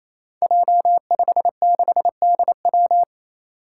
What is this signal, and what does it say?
Morse code, heard as a keyed tone, about 29 wpm, 700 Hz. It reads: J56DW
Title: Five Challenging Letter-Number Combinations 28wpm